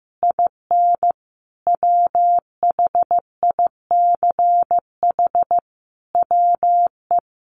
Morse code, heard as a keyed tone, about 15 wpm, 700 Hz. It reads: IN WHICH WE
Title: Sentences from Top 1400 Words 15wpm